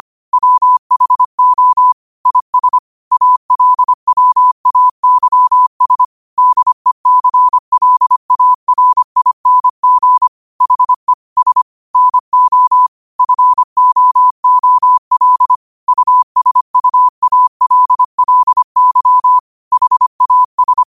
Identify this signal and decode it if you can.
Morse code, heard as a keyed tone, about 25 wpm, 1000 Hz. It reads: WHO IS ALWAYS DECLARING HE S NO FOOL USUALLY HAS